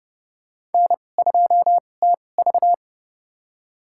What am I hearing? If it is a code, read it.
Morse code, heard as a keyed tone, about 30 wpm, spaced as for 22 wpm, 700 Hz. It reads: N2TV
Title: USA Call Signs - 4 Characters - 2x Character Spacing - 30wpm